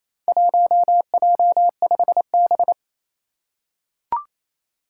Morse code, heard as a keyed tone, about 27 wpm, 700 Hz. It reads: 1J5B
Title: Four Challenging Letter-Number Combinations 28wpm